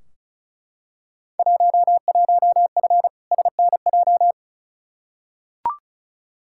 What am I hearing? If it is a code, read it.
Morse code, heard as a keyed tone, about 34 wpm, 700 Hz. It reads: 11F SNJ
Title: Field Day Exchanges Encore - 35wpm